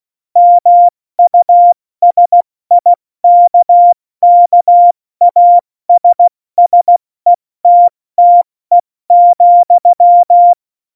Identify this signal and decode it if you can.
Morse code, heard as a keyed tone, about 16 wpm, 700 Hz. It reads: MUSIKKASSETTE,
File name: Morsekurs / Witze / Witz 17. Oktober 17, Tempo 70